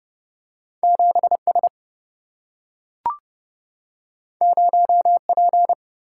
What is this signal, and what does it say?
Morse code, heard as a keyed tone, about 31 wpm, 700 Hz. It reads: 7S 0P
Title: Single-Letter Single-Number 30wpm